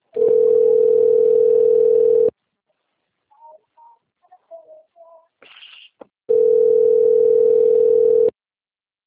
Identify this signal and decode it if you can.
Ringback tone (North America), 440 + 480 Hz.
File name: rg-1600-unknown- - - .53759